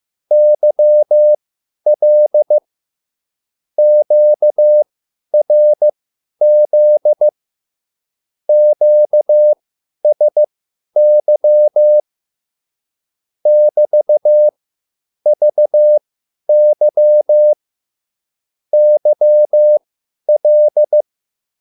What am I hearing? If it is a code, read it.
Morse code, heard as a keyed tone, about 15 wpm, spaced as for 10 wpm, 600 Hz. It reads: YL QRZ QSY <BT>VY YL
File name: QSL / CW lesson